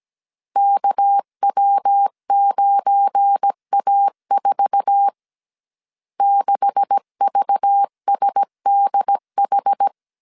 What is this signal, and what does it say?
Morse code, heard as a keyed tone, about 17 wpm, 800 Hz. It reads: KW9A4 6VSDH
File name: CW / DET / 1 9 1500C S A 16wpm 2828